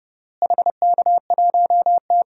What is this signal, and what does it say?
Morse code, heard as a keyed tone, about 30 wpm, 700 Hz. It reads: HK1T
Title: DXCC-100 Call Signs - 4 Characters - Repeated 3x - 30wpm